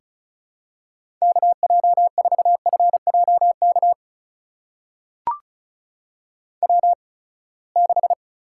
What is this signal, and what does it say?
Morse code, heard as a keyed tone, about 36 wpm, spaced as for 24 wpm, 700 Hz. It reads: KJ4FJK W6
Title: USA Call Signs - 8x Character Spacing - 35wpm